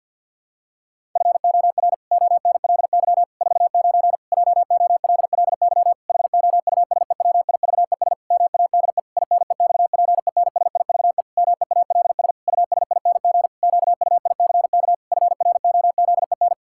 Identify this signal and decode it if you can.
Morse code, heard as a keyed tone, about 51 wpm, 700 Hz. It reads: FOR ONLY 49 JOLLY HOUSEWIVES MADE INEXPENSIVE MEALS USING QUICK FROZEN
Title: Pangram Challenge v2 - 50wpm